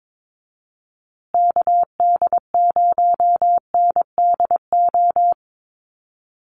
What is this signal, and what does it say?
Morse code, heard as a keyed tone, about 22 wpm, 700 Hz. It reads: KD0NDO
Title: USA Call Signs Repeated 3x 22wpm